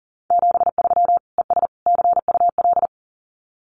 Morse code, heard as a keyed tone, about 40 wpm, 700 Hz. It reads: 73 ES CUL
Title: Common QSO Phrases Encore 40wpm